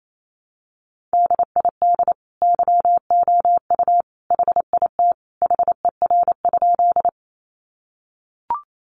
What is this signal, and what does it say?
Morse code, heard as a keyed tone, about 28 wpm, 700 Hz. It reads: DID YOU HIT HER?
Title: Sentences from Top 1200 Words 28wpm